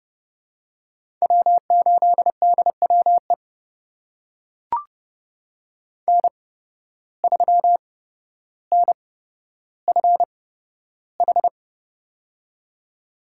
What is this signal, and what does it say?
Morse code, heard as a keyed tone, about 30 wpm, spaced as for 13 wpm, 700 Hz. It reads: W8DWE N3NFH
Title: USA Call Signs - 5 Characters - 8x Character Spacing - 30wpm